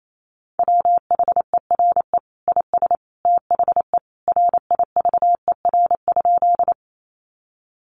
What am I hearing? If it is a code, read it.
Morse code, heard as a keyed tone, about 28 wpm, 700 Hz. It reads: WHERE IS THE RIVER?
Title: Sentences from Top 300 Words 28wpm